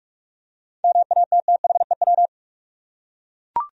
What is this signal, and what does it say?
Morse code, heard as a keyed tone, about 43 wpm, 700 Hz. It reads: MATTHEW
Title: USA Common Names - 45wpm